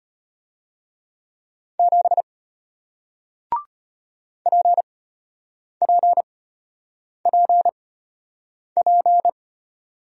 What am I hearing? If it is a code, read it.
Morse code, heard as a keyed tone, about 33 wpm, spaced as for 8 wpm, 700 Hz. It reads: 7 PPPP